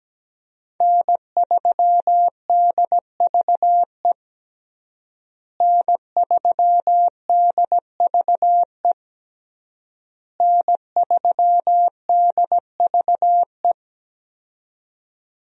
Morse code, heard as a keyed tone, about 17 wpm, 700 Hz. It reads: N3DVE N3DVE N3DVE
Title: USA Call Signs - 5 Characters - Repeated 3x - 17wpm